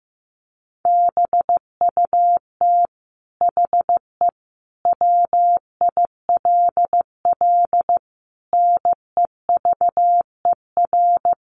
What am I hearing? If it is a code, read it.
Morse code, heard as a keyed tone, about 15 wpm, 700 Hz. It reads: BUT HE WILL NEVER
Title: Sentences from Top 1200 Words 15wpm